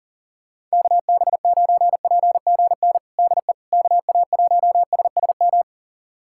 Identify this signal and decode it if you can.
Morse code, heard as a keyed tone, about 40 wpm, 700 Hz. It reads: KB9PGN DE KA1SSM
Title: USA Callsign DE Callsign - 6 Characters - 40wpm